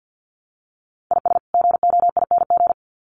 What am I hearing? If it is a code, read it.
Morse code, heard as a keyed tone, about 50 wpm, 700 Hz. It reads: IS GOING
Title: Sets of 2 Words 50wpm